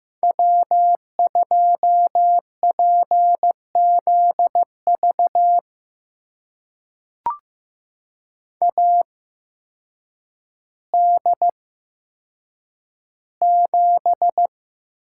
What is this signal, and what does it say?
Morse code, heard as a keyed tone, about 15 wpm, spaced as for 8 wpm, 700 Hz. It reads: W2PZV AD7
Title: USA Call Signs - 5 Characters - 8x Character Spacing - 15wpm